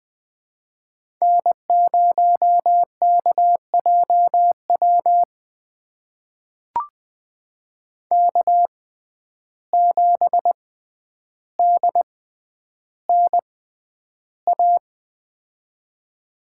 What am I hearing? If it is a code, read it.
Morse code, heard as a keyed tone, about 20 wpm, spaced as for 10 wpm, 700 Hz. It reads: N0KJW K7DNA
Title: USA Call Signs - 5 Characters - 6x Character Spacing - 20wpm